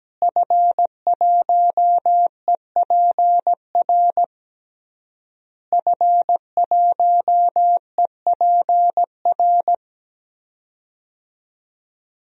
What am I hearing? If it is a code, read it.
Morse code, heard as a keyed tone, about 17 wpm, 700 Hz. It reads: F1EPR F1EPR
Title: DXCC-100 Call Signs - 5 Characters - Repeated 3x - 17wpm